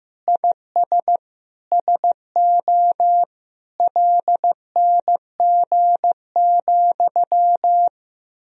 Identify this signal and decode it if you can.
Morse code, heard as a keyed tone, about 16 wpm, 700 Hz. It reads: IS SO LNG,
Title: Sentences from Top 500 Words and Abbreviations 15wpm